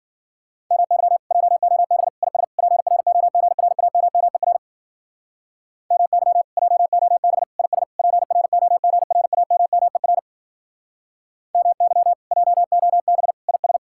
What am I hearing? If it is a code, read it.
Morse code, heard as a keyed tone, about 50 wpm, 700 Hz. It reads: MY JOB IS PROGRAMMER MY JOB IS PROGRAMMER MY JOB IS
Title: Speed-Racing Common QSO Phrases - 40wpm